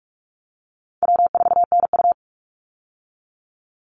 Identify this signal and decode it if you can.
Morse code, heard as a keyed tone, about 44 wpm, 700 Hz. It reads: W4NU
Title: Easy Call Signs Repeated 2x - 45wpm